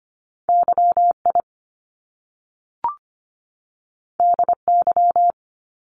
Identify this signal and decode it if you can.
Morse code, heard as a keyed tone, about 25 wpm, 700 Hz. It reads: Y I D Y
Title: Top Two Letter Combinations in Words 25wpm